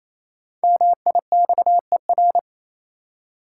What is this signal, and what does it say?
Morse code, heard as a keyed tone, about 27 wpm, 700 Hz. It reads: MIXER
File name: Easy Words - Speed-Racing with Hints - 22wpm